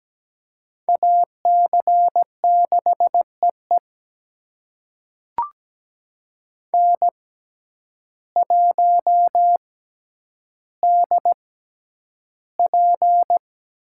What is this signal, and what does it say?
Morse code, heard as a keyed tone, about 17 wpm, spaced as for 9 wpm, 700 Hz. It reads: AC6EE N1DP
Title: USA Call Signs - 5 Characters - 6x Character Spacing - 17wpm